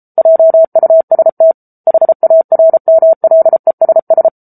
Morse code, heard as a keyed tone, about 34 wpm, 650 Hz. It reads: JUST HARMLESS
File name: Book Two / 07-Chapter 7 - 33 WPM